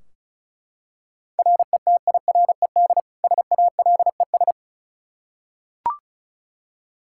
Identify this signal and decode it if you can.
Morse code, heard as a keyed tone, about 36 wpm, 700 Hz. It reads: RETIRED SALES
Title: Practice QSO - Exchange 3 - More Info - 35wpm